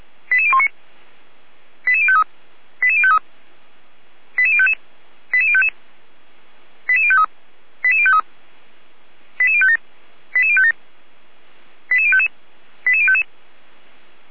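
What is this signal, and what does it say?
Five-tone selective call (ZVEI), heard as quick runs of five tones, tones of about 70 ms. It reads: 88019 88053 88053 88055 88055 88053 88053 88057 88057 88055 88055
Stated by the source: ZVEI five-tone call received by SDR, from the Feuerwehr Orlamünde (fire brigade) server audio.f85360.feuerwehr-orlamuende.org.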